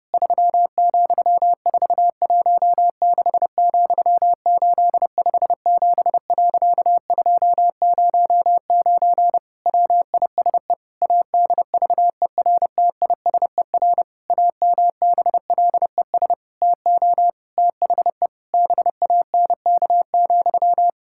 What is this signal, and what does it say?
Morse code, heard as a keyed tone, about 30 wpm, 700 Hz. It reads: 3,416,857.209 WISE ADVERTISER AMBLES TO THE BANK,
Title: Pangram Challenge v2 - 30wpm